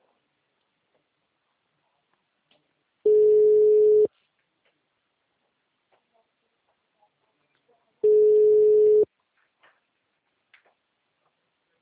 Ringback tone (Europe, 425 Hz).